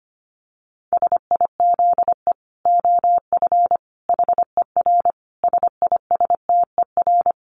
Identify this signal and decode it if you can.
Morse code, heard as a keyed tone, about 25 wpm, 700 Hz. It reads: SIZE OF HER SISTER
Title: Dit Challenge 25wpm